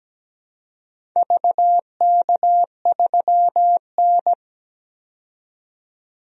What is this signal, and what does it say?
Morse code, heard as a keyed tone, about 17 wpm, 700 Hz. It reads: VK3N
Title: DXCC-100 Call Signs - 4 Characters - 17wpm